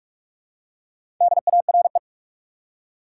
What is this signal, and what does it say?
Morse code, heard as a keyed tone, about 46 wpm, 700 Hz. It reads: DARE
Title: Four Letter Words - 45wpm